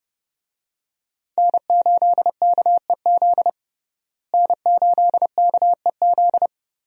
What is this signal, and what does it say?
Morse code, heard as a keyed tone, about 31 wpm, 700 Hz. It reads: N8KEZ N8KEZ
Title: USA Call Signs - 5 Characters - Repeated 2x - 30wpm